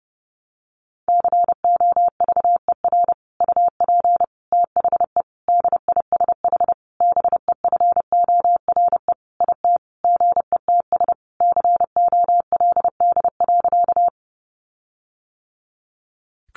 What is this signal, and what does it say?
Morse code, heard as a keyed tone, about 28 wpm, 700 Hz. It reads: COVER UP THE DISH BEFORE IT GETS COLD.
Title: Sentences from Top 1300 Words 30wpm